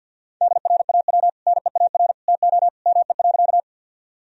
Morse code, heard as a keyed tone, about 46 wpm, 700 Hz. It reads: DRAW NEAR TO ME.
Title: Sentences from Top 1000 Words - Review Part 1 - 50wpm